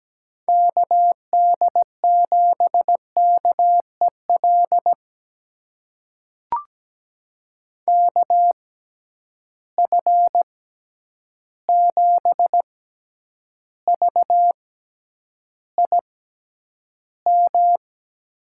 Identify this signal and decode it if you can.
Morse code, heard as a keyed tone, about 17 wpm, spaced as for 9 wpm, 700 Hz. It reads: KD7KEL KF7VIM
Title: USA Call Signs - 6 Characters - 6x Character Spacing - 17wpm